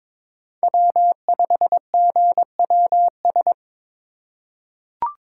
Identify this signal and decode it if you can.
Morse code, heard as a keyed tone, about 22 wpm, 700 Hz. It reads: W5GWS
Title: USA Call Signs - 5 Characters - 2x Character Spacing - 22wpm